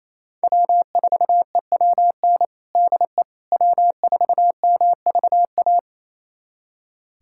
Morse code, heard as a keyed tone, about 28 wpm, 700 Hz. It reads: W4EWN DE W4MVA
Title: USA Callsign DE Callsign - 5 Characters - 28wpm